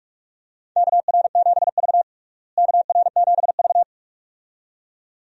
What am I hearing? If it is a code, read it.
Morse code, heard as a keyed tone, about 46 wpm, 700 Hz. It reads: KR7V KR7V
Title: USA Call Signs - 4 Characters - Repeated 2x - 45wpm